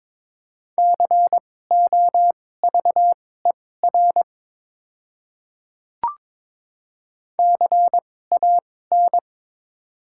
Morse code, heard as a keyed tone, about 22 wpm, spaced as for 16 wpm, 700 Hz. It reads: COVER CAN
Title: 2x Character Spacing - Top 500 Words - 22wpm